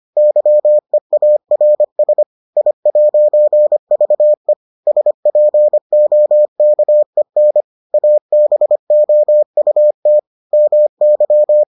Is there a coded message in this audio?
Yes: Morse code from 0.2 to 11.6 s, heard as a keyed tone, about 25 wpm, 600 Hz. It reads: YEARS I'VE SPOKEN ABOUT MY